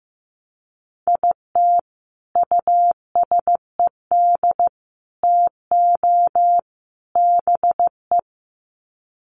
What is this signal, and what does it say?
Morse code, heard as a keyed tone, about 15 wpm, 700 Hz. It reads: IT USED TO BE
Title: Sets of 4 Words 15wpm